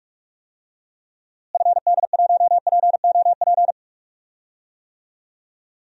Morse code, heard as a keyed tone, about 46 wpm, 700 Hz. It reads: FD1POP
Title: DXCC-100 Call Signs - 6 Characters - Repeated 2x - 45wpm